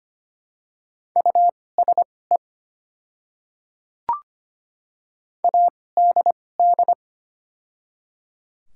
Morse code, heard as a keyed tone, about 25 wpm, spaced as for 18 wpm, 700 Hz. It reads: USE ADD
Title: 2x Character Spacing - Top 200 Words - 25wpm